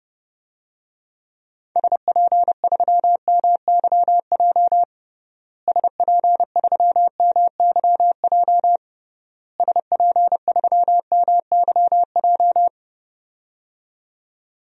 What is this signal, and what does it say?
Morse code, heard as a keyed tone, about 30 wpm, 700 Hz. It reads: SP3MYJ SP3MYJ SP3MYJ
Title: DXCC-100 Call Signs - 6 Characters - Repeated 3x - 30wpm